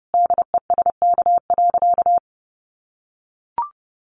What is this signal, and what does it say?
Morse code, heard as a keyed tone, about 30 wpm, 700 Hz. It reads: DESK.
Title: Additional Easy Phrases 30wpm